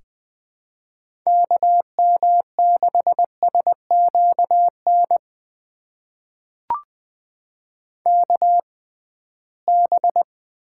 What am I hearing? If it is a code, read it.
Morse code, heard as a keyed tone, about 20 wpm, spaced as for 15 wpm, 700 Hz. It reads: KM6SQN KB
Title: USA Call Signs - 6 Characters - 6x Character Spacing - 20wpm